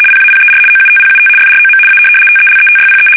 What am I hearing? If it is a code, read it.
RTTY, about 75 baud. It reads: PAPER OUTPUT FROM A TELETYPE
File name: fsk850-75B / teletype-corp